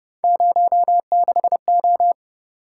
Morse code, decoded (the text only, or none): 06O